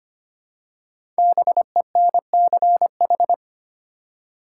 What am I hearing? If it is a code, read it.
Morse code, heard as a keyed tone, about 25 wpm, 700 Hz. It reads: BENCH
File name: 5 Letter Words - First 3 Letters Given - 25wpm